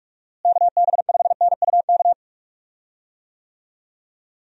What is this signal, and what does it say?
Morse code, heard as a keyed tone, about 42 wpm, 700 Hz. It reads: KB5NUK
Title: USA Call Signs - 6 Characters - Repeated 2x - 45wpm